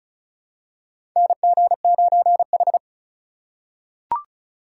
Morse code, heard as a keyed tone, about 36 wpm, 700 Hz. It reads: NG9H
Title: USA Call Signs - 4 Characters - 2x Character Spacing - 35wpm